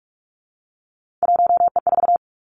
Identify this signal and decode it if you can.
Morse code, heard as a keyed tone, about 46 wpm, 700 Hz. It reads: 1E4
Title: Three Challenging Letter-Number Combinations 45wpm